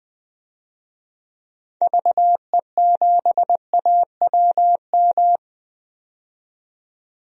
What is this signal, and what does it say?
Morse code, heard as a keyed tone, about 21 wpm, 700 Hz. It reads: VE7AWM
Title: DXCC-100 Call Signs - 6 Characters - 20wpm